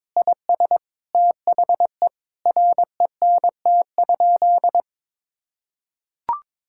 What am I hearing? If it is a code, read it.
Morse code, heard as a keyed tone, about 22 wpm, 700 Hz. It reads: IS THE RENT?
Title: Sentences from Top 1000 Words - Review Part 5 - 22wpm